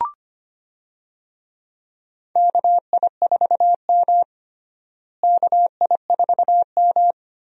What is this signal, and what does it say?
Morse code, heard as a keyed tone, about 25 wpm, 700 Hz. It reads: KI4M KI4M
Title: USA Call Signs - 4 Characters - Repeated 2x - 25wpm